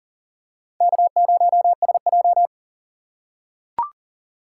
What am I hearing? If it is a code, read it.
Morse code, heard as a keyed tone, about 40 wpm, 700 Hz. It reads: K0SJ
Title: USA Call Signs - 4 Characters - 2x Character Spacing - 40wpm